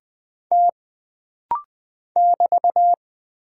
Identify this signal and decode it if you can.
Morse code, heard as a keyed tone, about 20 wpm, 700 Hz. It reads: T <BT>
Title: Single Character (Letter:Number:Punctuation:Prosign) - Rapid-Fire - 20wpm